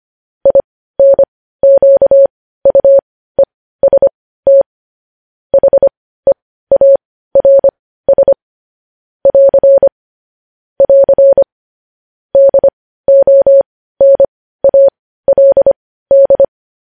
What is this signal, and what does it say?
Morse code, heard as a keyed tone, about 25 wpm, spaced as for 15 wpm, 550 Hz. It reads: INQUEST HEARS <AR> <AR> DONALD
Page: news headlines in morse code at 15 wpm